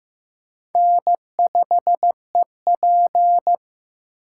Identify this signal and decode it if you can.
Morse code, heard as a keyed tone, about 15 wpm, 700 Hz. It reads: N5EP
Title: USA Call Signs - 2x Character Spacing - 15wpm